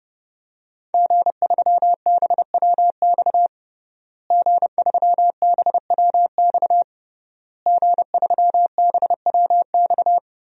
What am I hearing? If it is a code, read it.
Morse code, heard as a keyed tone, about 30 wpm, 700 Hz. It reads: G3BWX G3BWX G3BWX
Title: DXCC-100 Call Signs - 5 Characters - Repeated 3x - 30wpm